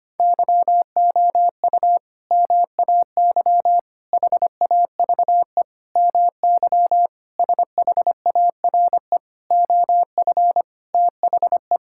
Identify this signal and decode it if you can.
Morse code, heard as a keyed tone, about 25 wpm, 700 Hz. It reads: YOU MAY HAVE MY SHARE OF THE
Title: Sentences from Top 1000 Words - Review Part 6 - 25wpm